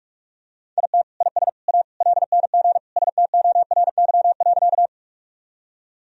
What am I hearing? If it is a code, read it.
Morse code, heard as a keyed tone, about 44 wpm, 700 Hz. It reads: IT IS A LNG STORY.
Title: Sentences from Top 300 Words and Abbreviations 45wpm